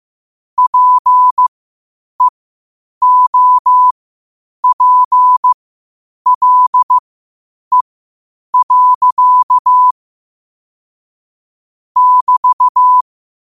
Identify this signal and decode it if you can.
Morse code, heard as a keyed tone, about 15 wpm, spaced as for 8 wpm, 1000 Hz. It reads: PEOPLE. <BT>
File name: qotdfiles / 8wpm